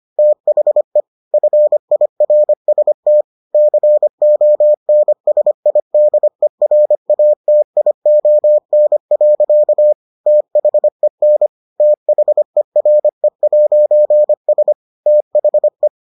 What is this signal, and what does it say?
Morse code, heard as a keyed tone, about 25 wpm, 600 Hz. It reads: THE FIRST CONSIDERATION. THEN THERE'S THE